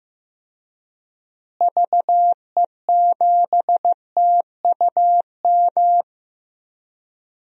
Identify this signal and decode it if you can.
Morse code, heard as a keyed tone, about 16 wpm, 700 Hz. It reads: VE7TUM
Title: DXCC-100 Call Signs - 6 Characters - Repeated 2x - 15wpm